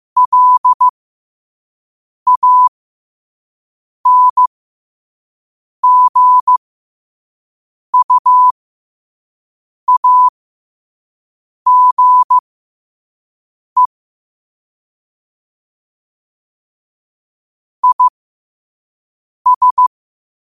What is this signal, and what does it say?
Morse code, heard as a keyed tone, about 15 wpm, spaced as for 5 wpm, 1000 Hz. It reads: LANGUAGE IS
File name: qotdfiles / 5wpm